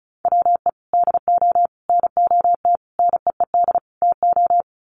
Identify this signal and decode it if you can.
Morse code, heard as a keyed tone, about 34 wpm, 700 Hz. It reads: WE DO NOT NEED TO
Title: Sets of 5 Words Encore 35wpm